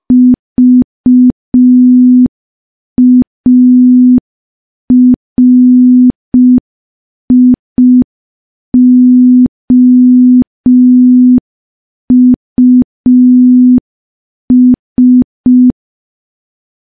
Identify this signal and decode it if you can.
Morse code, heard as a keyed tone, about 5 wpm, 250 Hz. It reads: VARIOUS
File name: morse / morse-training-05WPM